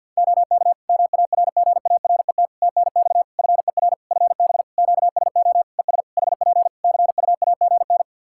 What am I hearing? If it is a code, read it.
Morse code, heard as a keyed tone, about 51 wpm, 700 Hz. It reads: OK MARGARET TNX FER FB QSO ES HP CUAGN
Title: Practice QSO - Exchange 4 - Ending - 50wpm